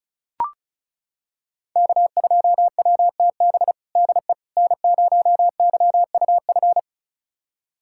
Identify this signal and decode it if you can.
Morse code, heard as a keyed tone, about 34 wpm, 700 Hz. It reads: K2WTB DE N0YUF
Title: USA Callsign DE Callsign - 5 Characters - 35wpm